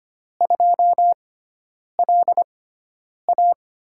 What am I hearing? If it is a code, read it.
Morse code, heard as a keyed tone, about 25 wpm, spaced as for 9 wpm, 700 Hz. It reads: 2LA
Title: USA Call Signs - 6x Character Spacing - 25wpm